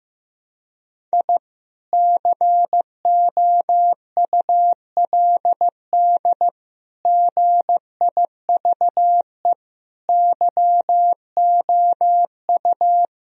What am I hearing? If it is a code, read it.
Morse code, heard as a keyed tone, about 14 wpm, 700 Hz. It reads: I COULD GIVE YOU
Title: Sentences from Top 1000 Words - Review Part 1 - 15wpm